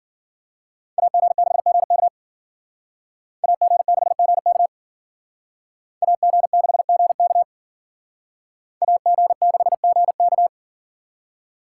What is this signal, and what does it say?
Morse code, heard as a keyed tone, about 51 wpm, 700 Hz. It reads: AG6GK AG6GK AG6GK AG6GK